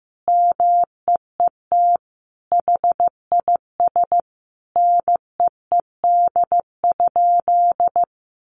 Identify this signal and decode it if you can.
Morse code, heard as a keyed tone, about 15 wpm, 700 Hz. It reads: MEET HIS NEED?
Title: Sentences from Top 1100 Words 15wpm